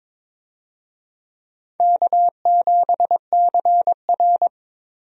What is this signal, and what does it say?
Morse code, heard as a keyed tone, about 22 wpm, 700 Hz. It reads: K7CR
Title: USA Call Signs - 4 Characters - 22wpm